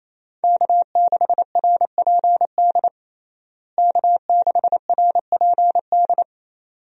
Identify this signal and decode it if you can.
Morse code, heard as a keyed tone, about 28 wpm, 700 Hz. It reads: K6RPD K6RPD
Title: USA Call Signs - 5 Characters - Repeated 2x - 28wpm